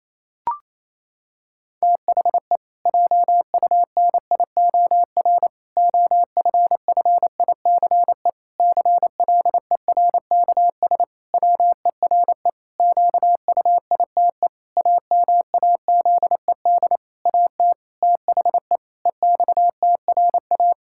Morse code, heard as a keyed tone, about 28 wpm, 700 Hz. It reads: THE JUNIOR OFFICE CLERKS WERE QUITE AMAZED AT THE EXTRA
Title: Pangram Challenge v2 - 28wpm